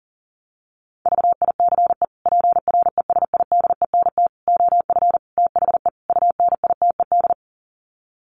Morse code, heard as a keyed tone, about 40 wpm, 700 Hz. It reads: VICE PRESIDENT OF THE UNITED
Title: Sets of 5 Words Encore2 40wpm